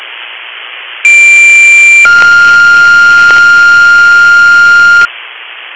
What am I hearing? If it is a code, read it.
Two-tone page, heard as a one-second tone, then a three-second tone, about 2350 then 1400 Hz, with noise. It is heard from 1.0 to 5.1 s.